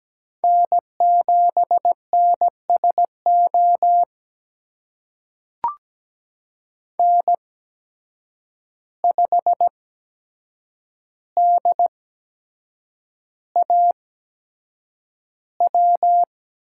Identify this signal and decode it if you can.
Morse code, heard as a keyed tone, about 17 wpm, spaced as for 7 wpm, 700 Hz. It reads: N7NSO N5DAW
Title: USA Call Signs - 5 Characters - 8x Character Spacing - 17wpm